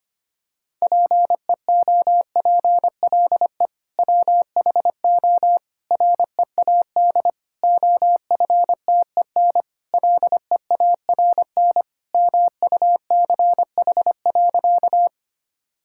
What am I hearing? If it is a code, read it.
Morse code, heard as a keyed tone, about 25 wpm, 700 Hz. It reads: PEOPLE WHO READ OFTEN LEARN MUCH.
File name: practice / 25wpm / Sentences with 2-Word Phrases - 25wpm